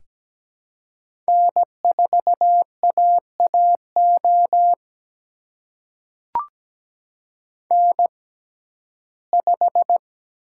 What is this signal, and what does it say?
Morse code, heard as a keyed tone, about 17 wpm, spaced as for 12 wpm, 700 Hz. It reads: N4AAO N5
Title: USA Call Signs - 5 Characters - 6x Character Spacing - 17wpm